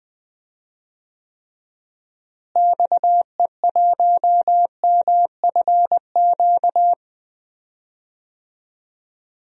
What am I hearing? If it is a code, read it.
Morse code, heard as a keyed tone, about 20 wpm, 700 Hz. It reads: XE1MFQ
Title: DXCC-100 Call Signs - 6 Characters - Repeated 2x - 20wpm